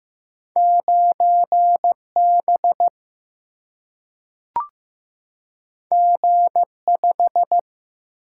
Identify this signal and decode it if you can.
Morse code, heard as a keyed tone, about 15 wpm, 700 Hz. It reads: 9B G5
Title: Single-Letter Single-Number 15wpm